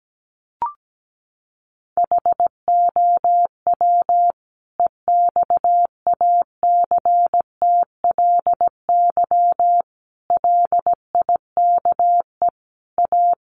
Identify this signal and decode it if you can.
Morse code, heard as a keyed tone, about 17 wpm, 700 Hz. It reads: HOW EXACTLY LIKE A
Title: Sentences from Top 900 Words 17wpm